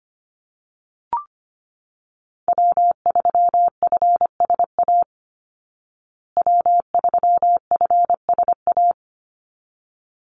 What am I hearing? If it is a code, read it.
Morse code, heard as a keyed tone, about 25 wpm, 700 Hz. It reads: W3FSA W3FSA
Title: Call Signs Repeated 3x - 25wpm